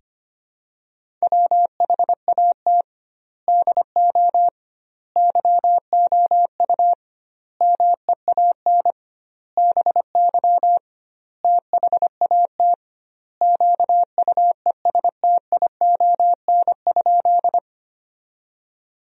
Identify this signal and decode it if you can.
Morse code, heard as a keyed tone, about 25 wpm, 700 Hz. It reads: WHAT DO YOU MEAN BY THAT QUESTION?
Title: 2x Word-Spacing - Sentences from Top 400 Words - 25wpm